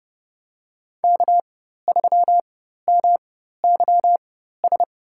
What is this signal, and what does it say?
Morse code, heard as a keyed tone, about 30 wpm, spaced as for 14 wpm, 700 Hz. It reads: K3MYS
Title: USA Call Signs - 5 Characters - 4x Character Spacing - 30wpm